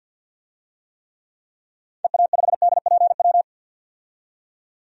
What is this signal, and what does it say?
Morse code, heard as a keyed tone, about 50 wpm, 700 Hz. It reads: EA5DPW
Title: DXCC-100 Call Signs - 6 Characters - 50wpm